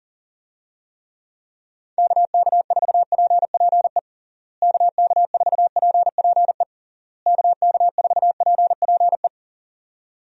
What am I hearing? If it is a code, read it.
Morse code, heard as a keyed tone, about 40 wpm, 700 Hz. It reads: KK4PPE KK4PPE KK4PPE
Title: USA Call Signs - 6 Characters - Repeated 3x - 40wpm